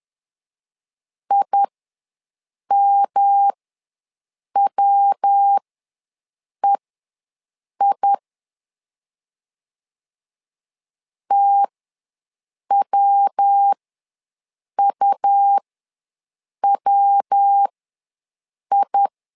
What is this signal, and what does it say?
Morse code, heard as a keyed tone, about 10 wpm, spaced as for 6 wpm, 800 Hz. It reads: IMWEI TWUWI
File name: LEZIONI / 3 MWRO / 3 MW ws 10wpm 2213